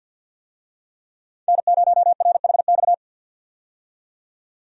Morse code, heard as a keyed tone, about 47 wpm, 700 Hz. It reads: N0RHX
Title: USA Call Signs - 5 Characters - 50wpm